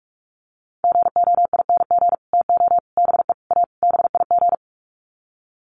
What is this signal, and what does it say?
Morse code, heard as a keyed tone, about 44 wpm, 700 Hz. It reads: GOING TO BE A BIG
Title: Sets of 5 Words Encore2 45wpm